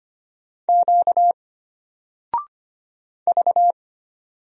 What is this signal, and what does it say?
Morse code, heard as a keyed tone, about 25 wpm, 700 Hz. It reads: Q V
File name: Single-Letter-Number-25wpm